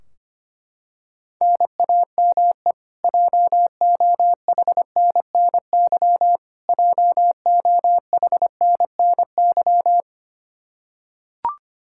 Morse code, heard as a keyed tone, about 25 wpm, 700 Hz. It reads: NAME JOHNNY JOHNNY
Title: Practice QSO - Exchange 1 - Beginning - 25wpm